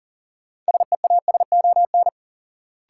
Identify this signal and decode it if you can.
Morse code, heard as a keyed tone, about 40 wpm, 700 Hz. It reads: SEASON
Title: Sets of 5 Words - New Word Drill - 40wpm